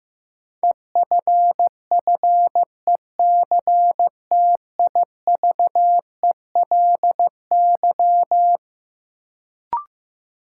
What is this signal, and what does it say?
Morse code, heard as a keyed tone, about 15 wpm, 700 Hz. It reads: EFFECTIVELY
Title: Sentences from Top 1400 Words - New Word Drill - 15wpm